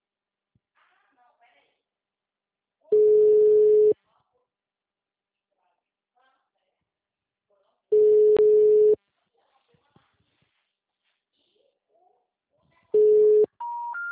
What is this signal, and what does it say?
Ringback tone (Europe, 425 Hz), with speech.